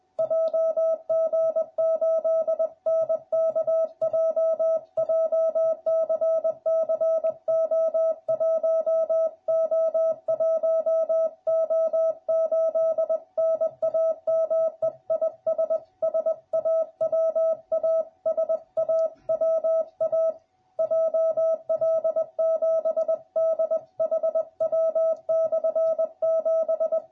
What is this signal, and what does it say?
Morse code, heard as a keyed tone, about 20 wpm, 650 Hz. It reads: JG8NKJ JCC O1O1O8 NAME IS SAWA SAWA JL7DHW/7